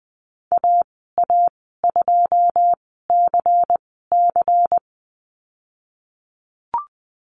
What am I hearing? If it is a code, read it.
Morse code, heard as a keyed tone, about 20 wpm, spaced as for 15 wpm, 700 Hz. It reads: AA2CC
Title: 154 Lesson 25g - Letter p - Call Signs 20wpm